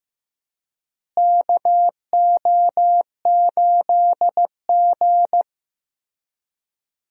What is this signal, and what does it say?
Morse code, heard as a keyed tone, about 15 wpm, 700 Hz. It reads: KO8G
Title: USA Call Signs - 4 Characters - Repeated 2x - 15wpm